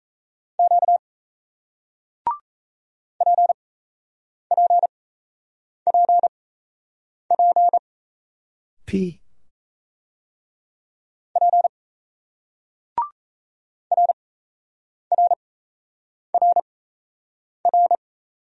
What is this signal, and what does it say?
Morse code, heard as a keyed tone, about 38 wpm, spaced as for 8 wpm, 700 Hz. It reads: Q PPPP P RRRR